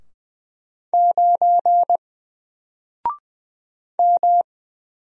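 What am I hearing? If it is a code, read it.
Morse code, heard as a keyed tone, about 20 wpm, 700 Hz. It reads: 9 M
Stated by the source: Single letter and number Morse drill at 20 wpm, from N4LCD.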